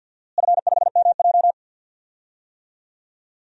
Morse code, heard as a keyed tone, about 53 wpm, 700 Hz. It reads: F5MJ
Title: DXCC-100 Call Signs - 4 Characters - Repeated 2x - 50wpm